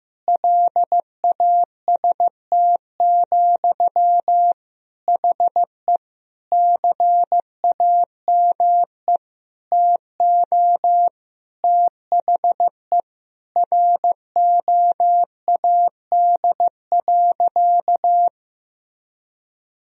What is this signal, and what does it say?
Morse code, heard as a keyed tone, about 15 wpm, 700 Hz. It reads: LAST, HE CAME TO THE ROAD.
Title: Sentences from Top 1000 Words - Review Part 2 - 15wpm